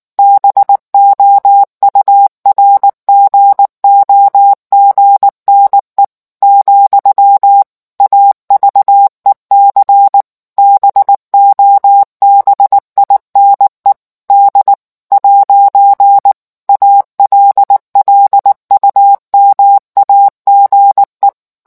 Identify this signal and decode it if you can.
Morse code, heard as a keyed tone, about 19 wpm, 800 Hz. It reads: BOURGOGNE , AVEC BOBINE D ' ALLUMAGE